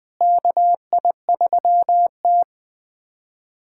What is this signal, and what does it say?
Morse code, heard as a keyed tone, about 20 wpm, 700 Hz. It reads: KI3T
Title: USA Callsign DE Callsign - 4 Characters - 20wpm